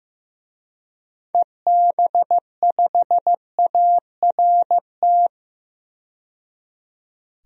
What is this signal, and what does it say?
Morse code, heard as a keyed tone, about 15 wpm, 700 Hz. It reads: EB5ART
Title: DXCC-100 Call Signs - 6 Characters - 15wpm